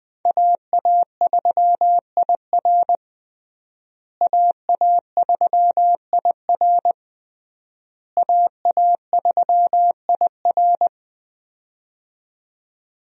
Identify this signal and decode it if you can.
Morse code, heard as a keyed tone, about 21 wpm, 700 Hz. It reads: AA3IR AA3IR AA3IR
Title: USA Call Signs - 5 Characters - Repeated 3x - 20wpm